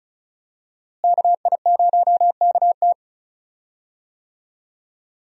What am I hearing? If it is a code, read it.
Morse code, heard as a keyed tone, about 36 wpm, 700 Hz. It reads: KI0KT
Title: USA Call Signs - 5 Characters - Repeated 3x - 35wpm